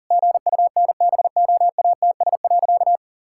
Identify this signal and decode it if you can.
Morse code, heard as a keyed tone, about 40 wpm, 700 Hz. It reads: GUNBOATS.
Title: Pangram Challenge v2 - 40wpm